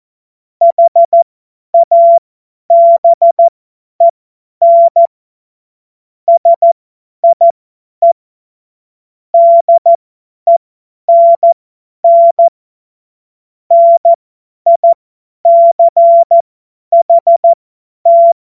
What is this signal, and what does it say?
Morse code, heard as a keyed tone, about 14 wpm, spaced as for 10 wpm, 680 Hz. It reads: HABEN SIE DENN NICHT